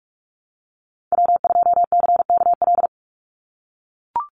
Morse code, heard as a keyed tone, about 44 wpm, 700 Hz. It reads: W2CKL
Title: NA SOTA Activator Callsigns 45wpm